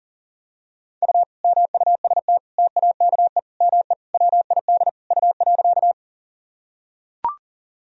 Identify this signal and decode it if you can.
Morse code, heard as a keyed tone, about 40 wpm, 700 Hz. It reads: U MUST TAKE ME WID U.
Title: Sentences from Top 200 Words and Abbreviations 40wpm